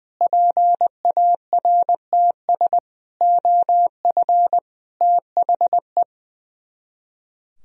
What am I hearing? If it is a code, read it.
Morse code, heard as a keyed tone, about 21 wpm, 700 Hz. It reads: PARTS OF THE
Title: Speed-Racing - Sets of 5 Words 20wpm